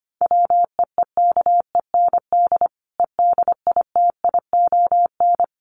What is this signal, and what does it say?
Morse code, heard as a keyed tone, about 25 wpm, 700 Hz. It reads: WEEKEND EDITION
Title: Sets of 5 Words 25wpm